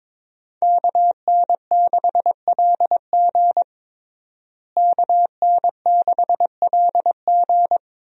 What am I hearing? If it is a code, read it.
Morse code, heard as a keyed tone, about 21 wpm, 700 Hz. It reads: KN6LG KN6LG
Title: USA Call Signs - 5 Characters - Repeated 2x - 22wpm